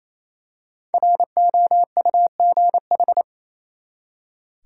Morse code, heard as a keyed tone, about 28 wpm, 700 Hz. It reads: ROUGH
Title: Five Letter Words - 28wpm